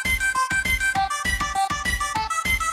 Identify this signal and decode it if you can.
There is no code.